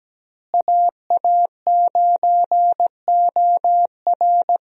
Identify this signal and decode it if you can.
Morse code, heard as a keyed tone, about 17 wpm, 700 Hz. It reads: AA9OR
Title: USA Call Signs - 5 Characters - Repeated 3x - 17wpm